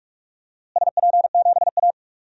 Morse code, heard as a keyed tone, about 46 wpm, 700 Hz. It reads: SP7A
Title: DXCC-100 Call Signs - 4 Characters - Repeated 3x - 45wpm